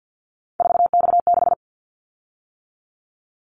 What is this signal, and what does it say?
Morse code, heard as a keyed tone, about 50 wpm, 700 Hz. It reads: 4X6
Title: Callsign Prefixes for Top 50 DXCC - 50wpm